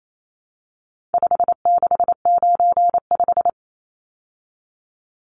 Morse code, heard as a keyed tone, about 28 wpm, 700 Hz. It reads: 5695
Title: Numbers - 4-Digit 28wpm